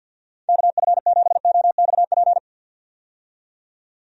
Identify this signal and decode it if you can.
Morse code, heard as a keyed tone, about 50 wpm, 700 Hz. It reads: KF7OXP
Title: USA Call Signs - 6 Characters - 50wpm